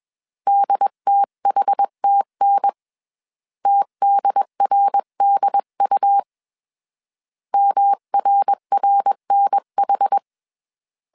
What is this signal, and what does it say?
Morse code, heard as a keyed tone, about 21 wpm, 800 Hz. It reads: DTHTN TDRDU MRRNH